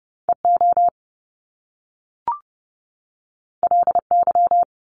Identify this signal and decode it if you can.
Morse code, heard as a keyed tone, about 30 wpm, 700 Hz. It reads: EO LY
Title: Top Two Letter Combinations in Words 30wpm